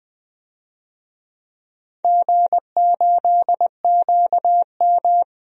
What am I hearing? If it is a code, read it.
Morse code, heard as a keyed tone, about 20 wpm, 700 Hz. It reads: G8QM
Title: DXCC-100 Call Signs - 4 Characters - 20wpm